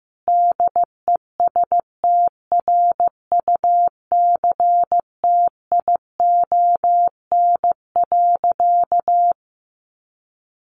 Morse code, heard as a keyed tone, about 15 wpm, 700 Hz. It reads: DESTRUCTION.
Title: A Sound Of Thunder 15wpm